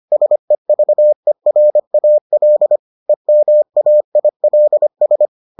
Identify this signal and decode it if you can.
Morse code, heard as a keyed tone, about 25 wpm, 600 Hz. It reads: SEVERAL EMAILS